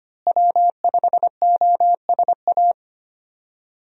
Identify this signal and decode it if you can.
Morse code, heard as a keyed tone, about 25 wpm, 700 Hz. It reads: W5OSA
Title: USA Call Signs - 5 Characters - Repeated 2x - 25wpm